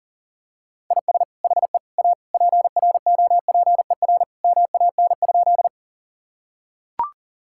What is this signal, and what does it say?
Morse code, heard as a keyed tone, about 40 wpm, 700 Hz. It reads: IS HE A PROPER MAN?
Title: Sentences from Top 1000 Words - Review Part 5 - 40wpm